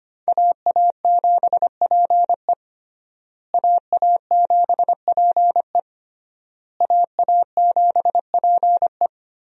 Morse code, heard as a keyed tone, about 25 wpm, 700 Hz. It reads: AA7PE AA7PE AA7PE
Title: USA Call Signs - 5 Characters - Repeated 3x - 25wpm